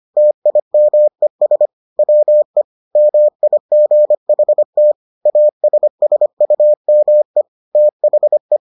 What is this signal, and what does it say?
Morse code, heard as a keyed tone, about 25 wpm, 600 Hz. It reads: TIMES WE MIGHT ASSUME THE